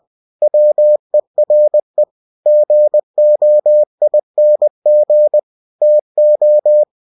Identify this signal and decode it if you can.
Morse code, heard as a keyed tone, about 20 wpm, 600 Hz. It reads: WERE GOING TO